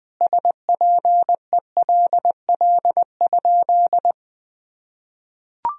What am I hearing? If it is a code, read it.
Morse code, heard as a keyed tone, about 21 wpm, 700 Hz. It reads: SPELL?
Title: Sentences from Top 1000 Words - Review Part 1 - 20wpm